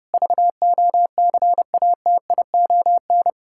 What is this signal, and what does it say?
Morse code, heard as a keyed tone, about 30 wpm, 700 Hz. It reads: VOCATION
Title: Pangram Challenge v2 - 30wpm